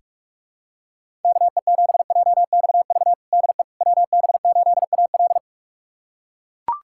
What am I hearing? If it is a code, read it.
Morse code, heard as a keyed tone, about 43 wpm, 700 Hz. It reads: KE7JXV DE WB8AL
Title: USA Callsign DE Callsign 45wpm